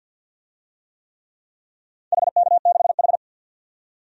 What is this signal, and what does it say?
Morse code, heard as a keyed tone, about 47 wpm, 700 Hz. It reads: HK6H
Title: DXCC-100 Call Signs - 4 Characters - 50wpm